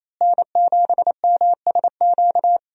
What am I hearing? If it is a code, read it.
Morse code, heard as a keyed tone, about 28 wpm, 700 Hz. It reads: N7MSQ
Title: USA Call Signs - 5 Characters - Repeated 3x - 28wpm